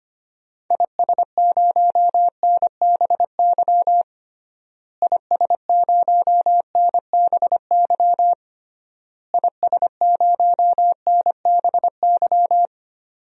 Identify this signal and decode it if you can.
Morse code, heard as a keyed tone, about 25 wpm, 700 Hz. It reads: IS0NBY IS0NBY IS0NBY
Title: DXCC-100 Call Signs - 6 Characters - Repeated 3x - 25wpm